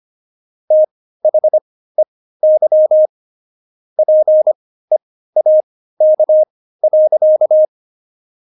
Morse code, heard as a keyed tone, about 25 wpm, spaced as for 15 wpm, 620 Hz. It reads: THEY PEAK.
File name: Jokes-25-char-15-eff